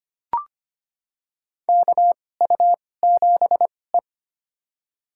Morse code, heard as a keyed tone, about 25 wpm, spaced as for 18 wpm, 700 Hz. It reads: KU7E
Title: USA Call Signs - 4 Characters - 2x Character Spacing - 25wpm